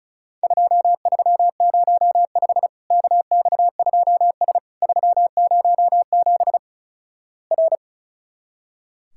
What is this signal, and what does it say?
Morse code, heard as a keyed tone, about 36 wpm, 700 Hz. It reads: 2305 KX2S 307 R
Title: Worked All Europe DX Contest - QTC Exchange - 35wpm